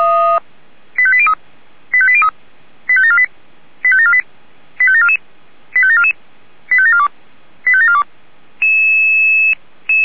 Five-tone selective call (ZVEI), heard as quick runs of five tones, tones of about 70 ms. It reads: 85803 85803 85748 85748 85744 85744 85742 85742